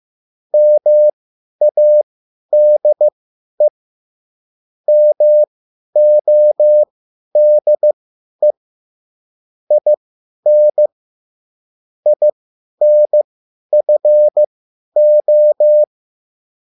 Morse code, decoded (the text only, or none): MADE MODE IN INFO